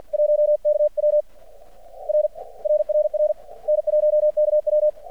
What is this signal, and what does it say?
Morse code, heard as a keyed tone, about 48 wpm, 600 Hz. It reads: 1KW R RRR N1KW
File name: archived / CRSnet